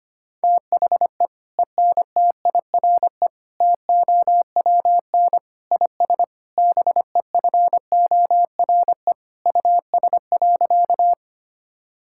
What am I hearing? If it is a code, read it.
Morse code, heard as a keyed tone, about 25 wpm, 700 Hz. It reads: THE ENTIRE TOWN IS BEFORE US.
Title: Sentences from Top 1000 Words - Review Part 5 - 25wpm